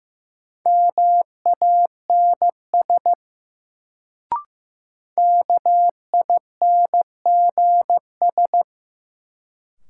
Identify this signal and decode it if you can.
Morse code, heard as a keyed tone, about 15 wpm, 700 Hz. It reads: MANS KINGS
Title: Sentences from Top 1000 Words - Review Part 6 - New Word Drill - 15wpm